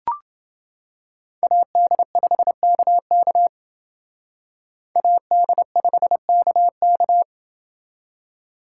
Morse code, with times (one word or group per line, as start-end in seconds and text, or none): 1.4-3.5 AD5KK
5.0-7.2 AD5KK